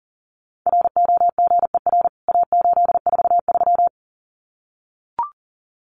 Morse code, heard as a keyed tone, about 40 wpm, 700 Hz. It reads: ROGER A843
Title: CWT Exchange 40wpm